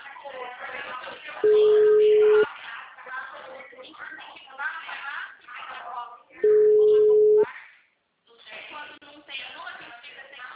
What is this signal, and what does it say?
Ringback tone (Europe, 425 Hz), with speech.